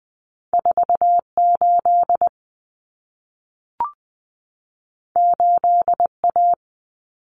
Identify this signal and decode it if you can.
Morse code, heard as a keyed tone, about 20 wpm, 700 Hz. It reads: 48 8A
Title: Common Cut Numbers - 2-Digit 20wpm